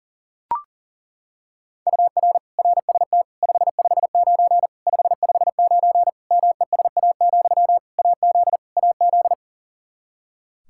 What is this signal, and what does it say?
Morse code, heard as a keyed tone, about 40 wpm, 700 Hz. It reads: UR RST 559 559 MESA, AZ AZ
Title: Practice QSO - Exchange 1 - Beginning - 40wpm